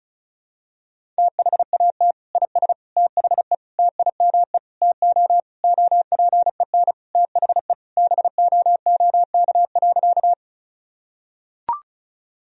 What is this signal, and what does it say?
Morse code, heard as a keyed tone, about 36 wpm, 700 Hz. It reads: THAT IS THE TIME TO OPEN THE BOOK.
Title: Sentences from Top 1000 Words - Review Part 1 - 35wpm